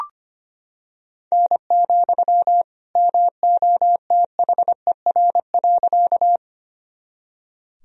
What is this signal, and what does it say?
Morse code, heard as a keyed tone, about 25 wpm, 700 Hz. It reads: N, MOTHER.
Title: Sentences from Top 200 Words and Abbreviations 25wpm